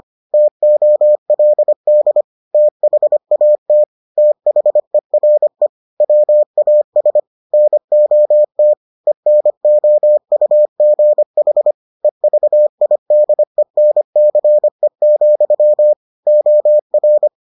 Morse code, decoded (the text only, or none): TOLD THAT THERE WAS NOT ENOUGH EVIDENCE, OR